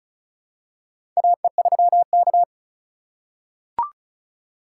Morse code, heard as a keyed tone, about 34 wpm, 700 Hz. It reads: AE3K
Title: USA Call Signs - 4 Characters - 2x Character Spacing - 35wpm